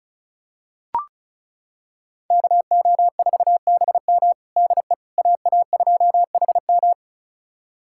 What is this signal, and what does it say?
Morse code, heard as a keyed tone, about 36 wpm, 700 Hz. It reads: KO4BM DE AA2HM
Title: USA Callsign DE Callsign - 5 Characters - 35wpm